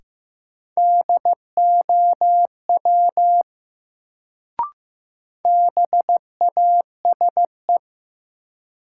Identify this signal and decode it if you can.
Morse code, heard as a keyed tone, about 15 wpm, 700 Hz. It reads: DOW BASE
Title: Sets of 5 Words - New Word Drill - 15wpm